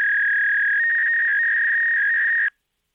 Fax machine handshake.